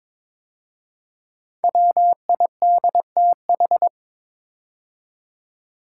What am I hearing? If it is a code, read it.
Morse code, heard as a keyed tone, about 22 wpm, 700 Hz. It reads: WIDTH
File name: 5 Letter Words - First 3 Letters Given - 22wpm